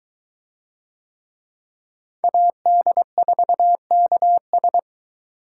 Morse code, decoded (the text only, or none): AD4KS